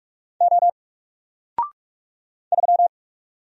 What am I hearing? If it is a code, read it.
Morse code, heard as a keyed tone, about 44 wpm, 700 Hz. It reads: O 3
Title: Single Letter-Number 45wpm